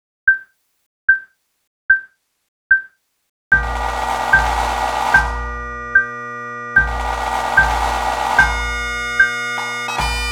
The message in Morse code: none